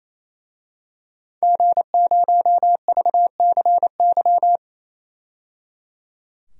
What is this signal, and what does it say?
Morse code, heard as a keyed tone, about 28 wpm, 700 Hz. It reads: G0VCY
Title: DXCC-100 Call Signs - 5 Characters - 28wpm